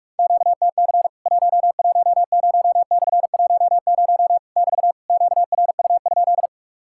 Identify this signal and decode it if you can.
Morse code, heard as a keyed tone, about 46 wpm, 700 Hz. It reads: QTC 110/10 <BT> QRU?
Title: Worked All Europe DX Contest - QTC Exchange - 45wpm